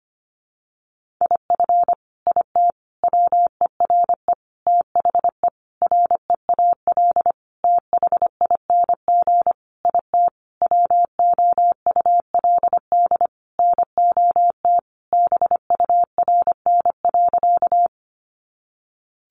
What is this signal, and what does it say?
Morse code, heard as a keyed tone, about 25 wpm, 700 Hz. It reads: IF IT WERE THE REAL THING IT WOULD NOT BURN.
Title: Sentences from Top 1300 Words 25wpm